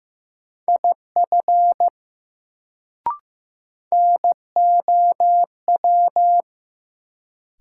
Morse code, heard as a keyed tone, about 15 wpm, 700 Hz. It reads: IF NOW
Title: Sets of 2 Words - New Word Drill - 15wpm